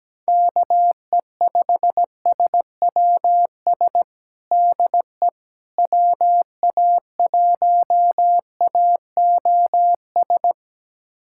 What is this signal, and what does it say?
Morse code, heard as a keyed tone, about 17 wpm, 700 Hz. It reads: KE5SWS DE WA1AOS
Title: USA Callsign DE Callsign - 6 Characters - 17wpm